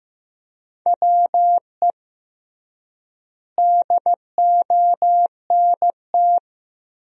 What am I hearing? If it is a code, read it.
Morse code, heard as a keyed tone, about 15 wpm, 700 Hz. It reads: WE DONT
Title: 3x Word-Spacing - Sentences from Top 300 Words - 15wpm